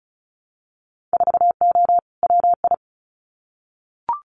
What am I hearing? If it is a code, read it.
Morse code, heard as a keyed tone, about 36 wpm, 700 Hz. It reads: 4O WI
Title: Winter Field Day Exchanges 35wpm